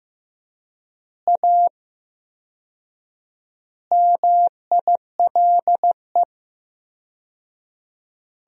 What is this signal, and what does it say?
Morse code, heard as a keyed tone, about 15 wpm, 700 Hz. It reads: A MILE
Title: 4x Word-Spacing - Sentences from Top 500 Words - 15wpm